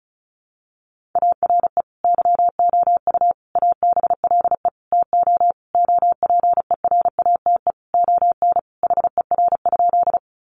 Morse code, heard as a keyed tone, about 34 wpm, 700 Hz. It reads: ARE YOU ABLE TO OPERATE ON HER?
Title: Sentences from Top 1500 Words 35wpm